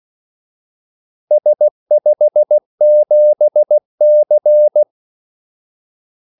Morse code, heard as a keyed tone, about 16 wpm, 600 Hz. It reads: S57C
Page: Kaldesignaler 16 wpm